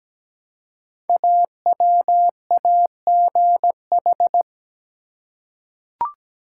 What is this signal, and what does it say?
Morse code, heard as a keyed tone, about 17 wpm, 700 Hz. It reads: AWAGH
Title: Code Groups - 5 Letter:Numbers - 17wpm